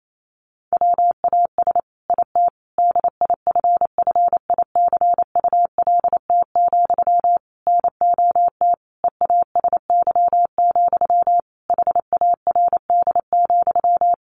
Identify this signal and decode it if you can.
Morse code, heard as a keyed tone, about 28 wpm, 700 Hz. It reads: WAS IT DIFFICULT, NOT EASY, HARD,
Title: Sentences from Top 1200 Words 28wpm